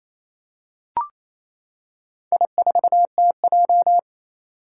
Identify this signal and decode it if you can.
Morse code, heard as a keyed tone, about 27 wpm, 700 Hz. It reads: I4TJ
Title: Four Challenging Letter-Number Combinations 28wpm